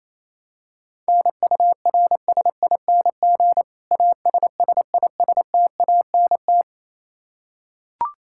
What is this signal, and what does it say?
Morse code, heard as a keyed tone, about 28 wpm, 700 Hz. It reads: NURSING ASSISTANT
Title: Practice QSO - Exchange 3 - More Info - 28wpm